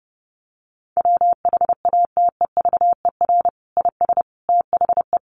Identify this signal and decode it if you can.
Morse code, heard as a keyed tone, about 30 wpm, 700 Hz. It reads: WHATEVER IS THE
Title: Sentences from Top 1100 Words 30wpm